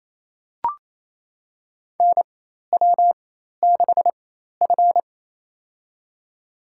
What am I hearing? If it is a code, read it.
Morse code, heard as a keyed tone, about 28 wpm, spaced as for 13 wpm, 700 Hz. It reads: NW6F
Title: USA Call Signs - 4x Character Spacing - 28wpm